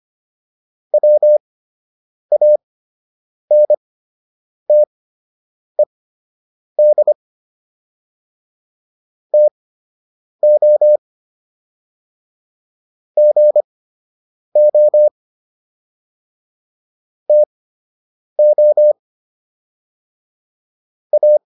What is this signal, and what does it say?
Morse code, heard as a keyed tone, about 25 wpm, spaced as for 8 wpm, 600 Hz. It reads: WANTED TO GO TO A